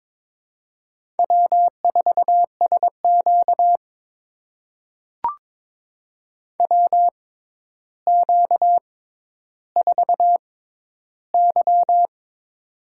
Morse code, heard as a keyed tone, about 21 wpm, spaced as for 11 wpm, 700 Hz. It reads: W4SQ WQ4Y
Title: USA Call Signs - 4 Characters - 6x Character Spacing - 22wpm